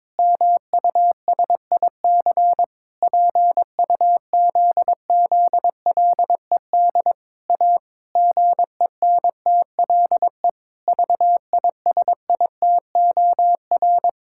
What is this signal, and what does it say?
Morse code, heard as a keyed tone, about 22 wpm, 700 Hz. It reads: MUSIC PUZZLED A GENTLE VISITOR
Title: Pangram Challenge v2 - 22wpm